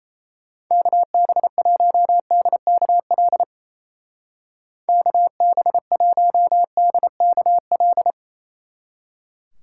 Morse code, heard as a keyed tone, about 36 wpm, 700 Hz. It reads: KB1DKL KB1DKL